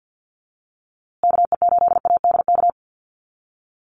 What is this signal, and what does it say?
Morse code, heard as a keyed tone, about 50 wpm, 700 Hz. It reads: KE8ADK
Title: USA Call Signs Repeated 2x 50wpm